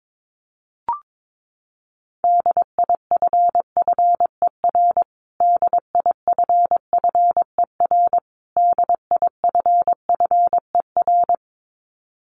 Morse code, heard as a keyed tone, about 22 wpm, 700 Hz. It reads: DIFFER DIFFER DIFFER
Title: Top 500 Words Repeated 22wpm